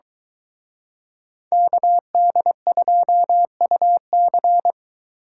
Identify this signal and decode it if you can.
Morse code, heard as a keyed tone, about 23 wpm, 700 Hz. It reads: KD2UC